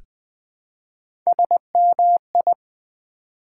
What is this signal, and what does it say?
Morse code, heard as a keyed tone, about 20 wpm, 700 Hz. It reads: SMI
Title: Build A Word - 1 Letter At A Time - 5-Letter Words - 20wpm